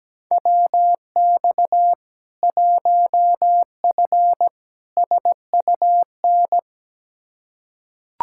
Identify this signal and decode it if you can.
Morse code, heard as a keyed tone, about 17 wpm, 700 Hz. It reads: WX 1F SUN
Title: Practice QSO - Exchange 2 - Info - 17wpm